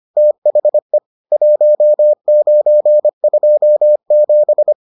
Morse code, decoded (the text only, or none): THE 1927